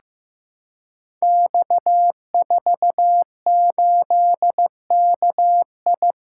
Morse code, heard as a keyed tone, about 15 wpm, 700 Hz. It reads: X48KI
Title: Code Groups - 5 Letter:Numbers - 15wpm